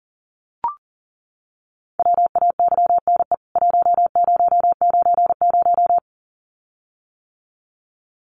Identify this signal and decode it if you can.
Morse code, heard as a keyed tone, about 40 wpm, 700 Hz. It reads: WAYNE 1090
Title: CWT Exchange 40wpm